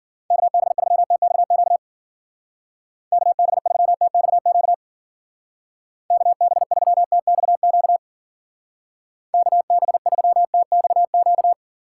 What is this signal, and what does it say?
Morse code, heard as a keyed tone, about 49 wpm, 700 Hz. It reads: KB3TXQ KB3TXQ KB3TXQ KB3TXQ
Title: Speed Racing - USA 2x3 Call Signs 40wpm